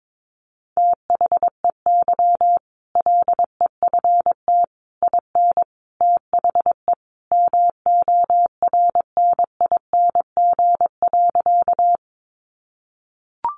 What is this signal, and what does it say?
Morse code, heard as a keyed tone, about 22 wpm, 700 Hz. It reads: THEY LEFT IN THE MORNING.
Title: Sentences from Top 400 Words 22wpm